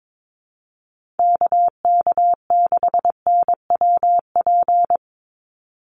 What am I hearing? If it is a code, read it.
Morse code, heard as a keyed tone, about 22 wpm, 700 Hz. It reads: KK6NWP
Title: USA Call Signs 22wpm